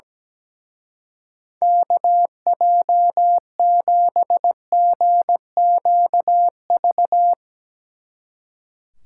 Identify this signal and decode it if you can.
Morse code, heard as a keyed tone, about 18 wpm, 700 Hz. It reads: KJ7GQV
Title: Speed Racing - USA Call Signs 17wpm